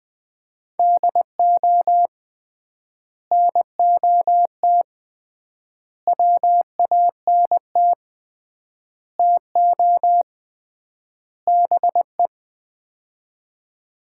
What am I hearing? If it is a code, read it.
Morse code, heard as a keyed tone, about 20 wpm, 700 Hz. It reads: DO NOT WANT TO BE
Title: 3x Word-Spacing - Sets of 5 Words - 20wpm